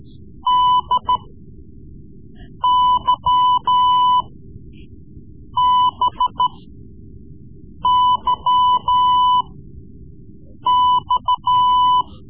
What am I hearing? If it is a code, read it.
Morse code, heard as a keyed tone, about 13 wpm, spaced as for 5 wpm, 1000 Hz. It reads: DYBYX
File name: Morse Lessons / Morse Lessons / Tape-3B